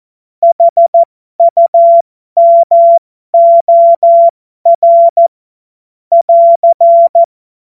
Morse code, heard as a keyed tone, about 14 wpm, spaced as for 12 wpm, 680 Hz. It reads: HUMOR <AR>
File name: Morsekurs / Witze